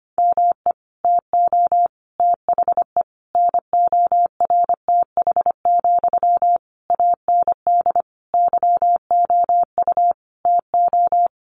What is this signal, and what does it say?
Morse code, heard as a keyed tone, about 25 wpm, 700 Hz. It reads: ME TO THE NORTH, AND YOU TO
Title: Sentences from Top 400 Words 25wpm